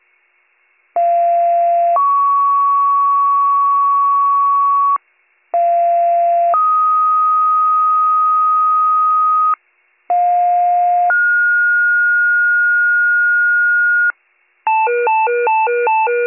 Two-tone page, heard as a one-second tone, then a three-second tone, about 690 then 1120 Hz, about 690 then 1250 Hz, about 710 then 1430 Hz. It is heard from 1.0 to 14.1 s.